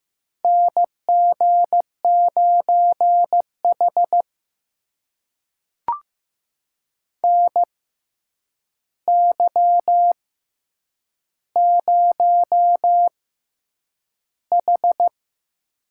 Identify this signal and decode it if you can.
Morse code, heard as a keyed tone, about 15 wpm, 700 Hz. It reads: N G 9 H N Y 0 H
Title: USA Call Signs - 4 Characters - 6x Character Spacing - 15wpm